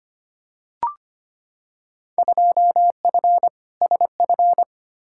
Morse code, heard as a keyed tone, about 25 wpm, 700 Hz. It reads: 2F SF
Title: Field Day Exchanges - 25wpm